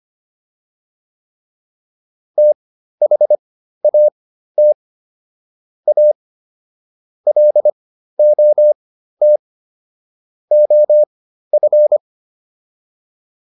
Morse code, heard as a keyed tone, about 25 wpm, spaced as for 13 wpm, 600 Hz. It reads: THAT A LOT OF
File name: WD302 13